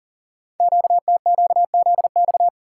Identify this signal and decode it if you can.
Morse code, heard as a keyed tone, about 42 wpm, 700 Hz. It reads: QTQZX
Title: Code Groups - 5 Letters - 40wpm